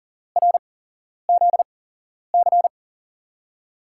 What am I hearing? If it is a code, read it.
Morse code, heard as a keyed tone, about 40 wpm, spaced as for 11 wpm, 700 Hz. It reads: RZC
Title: USA Call Signs - 5 Characters - 8x Character Spacing - 40wpm